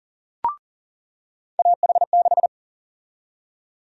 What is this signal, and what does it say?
Morse code, heard as a keyed tone, about 37 wpm, 700 Hz. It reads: AH6
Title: USA Call Sign Prefixes - 40wpm